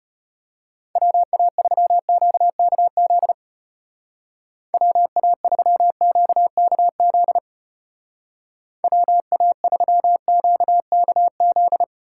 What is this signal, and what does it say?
Morse code, heard as a keyed tone, about 31 wpm, 700 Hz. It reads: WA3QKZ WA3QKZ WA3QKZ
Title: Speed Racing - USA 2x3 Call Signs 25wpm